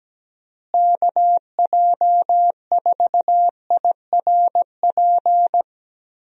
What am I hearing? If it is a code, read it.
Morse code, heard as a keyed tone, about 17 wpm, 700 Hz. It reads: KJ4IRP
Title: USA Call Signs - 6 Characters - Repeated 3x - 17wpm